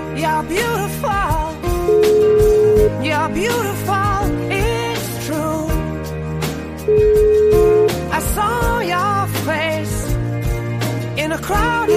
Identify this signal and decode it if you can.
Ringback tone (Europe, 425 Hz), with noise.